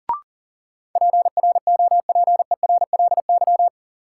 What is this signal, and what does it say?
Morse code, heard as a keyed tone, about 37 wpm, 700 Hz. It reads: PROPERLY
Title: Sentences from Top 1100 Words - New Word Drill Alt - 40wpm